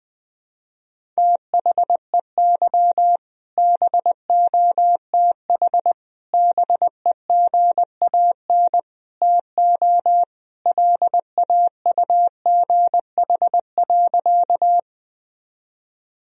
Morse code, heard as a keyed tone, about 20 wpm, 700 Hz. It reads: THEY BOTH BEGAN TO LAUGH.
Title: Sentences from Top 1000 Words - Review Part 1 - 20wpm